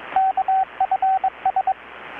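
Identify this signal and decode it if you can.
Morse code, heard as a keyed tone, about 21 wpm, 760 Hz, with radio noise. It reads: KFS
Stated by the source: Morse from KPH's Night of Nights XX maritime radiotelegraphy event, received in Chiapas, Mexico about 3,800 km away on a Kenwood R-600 with a grounded 10 m long-wire antenna.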